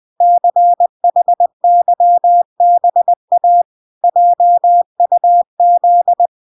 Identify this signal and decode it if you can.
Morse code, heard as a keyed tone, about 20 wpm, 700 Hz. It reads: CHYBA JUZ